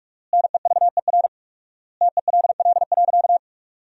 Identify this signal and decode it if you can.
Morse code, heard as a keyed tone, about 46 wpm, 700 Hz. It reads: NEVER TELL.
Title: 4x Word-Spacing - Sentences from Top 300 Words - 45wpm